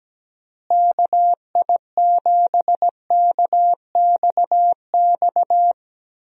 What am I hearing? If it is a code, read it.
Morse code, heard as a keyed tone, about 17 wpm, 700 Hz. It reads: KI7KXX
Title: USA Call Signs - 6 Characters - Repeated 3x - 17wpm